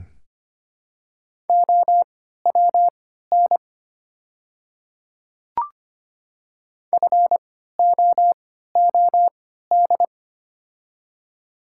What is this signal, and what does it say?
Morse code, heard as a keyed tone, about 25 wpm, spaced as for 14 wpm, 700 Hz. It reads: OWN FOOD
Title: 3x Character Spacing - Top 500 Words - 25wpm